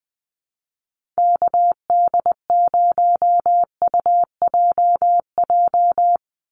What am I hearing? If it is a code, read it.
Morse code, heard as a keyed tone, about 20 wpm, 700 Hz. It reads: KD0UJJ
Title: USA Call Signs 20wpm